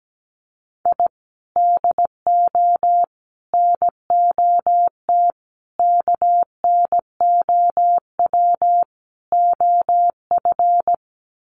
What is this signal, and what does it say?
Morse code, heard as a keyed tone, about 17 wpm, 700 Hz. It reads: I DO NOT KNOW OF
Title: Sets of 5 Words Encore2 17wpm